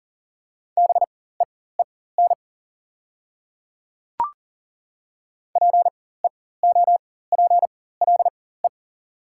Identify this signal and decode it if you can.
Morse code, heard as a keyed tone, about 39 wpm, spaced as for 19 wpm, 700 Hz. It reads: BEEN PEOPLE
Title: 4x Character Spacing - Top 100 Words - 40wpm